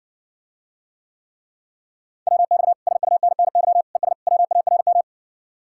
Morse code, heard as a keyed tone, about 61 wpm, 700 Hz. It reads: WX SUNNY ES WARM